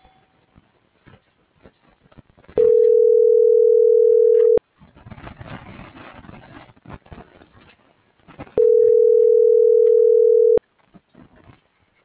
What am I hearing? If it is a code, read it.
Ringback tone (North America), 440 + 480 Hz.